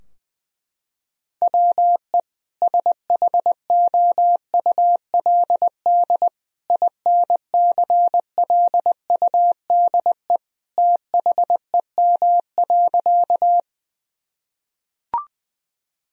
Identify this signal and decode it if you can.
Morse code, heard as a keyed tone, about 20 wpm, 700 Hz. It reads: WE SHOULD INCLUDE THEM.
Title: Sentences from Top 1000 Words - Review Part 3 - 20wpm